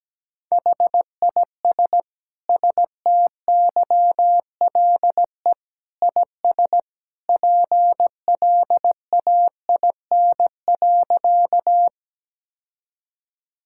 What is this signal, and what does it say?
Morse code, heard as a keyed tone, about 17 wpm, 700 Hz. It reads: HIS STYLE IS PLAIN.
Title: Sentences from Top 1000 Words - Review Part 5 - 17wpm